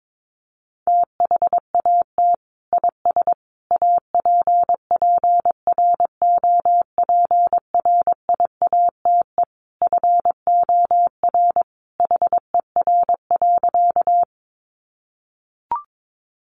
Morse code, heard as a keyed tone, about 22 wpm, 700 Hz. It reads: THAT IS APPROPRIATE FOR HER.
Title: Sentences from Top 900 Words 22wpm